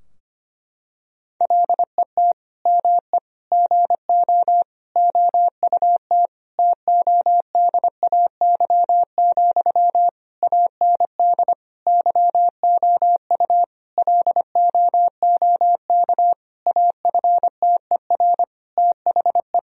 Morse code, heard as a keyed tone, about 25 wpm, 700 Hz. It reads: LET ME GO OUT TODAY, AND YOU LOOK AFTER THE
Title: Sentences from Top 1000 Words - Review Part 4 - 25wpm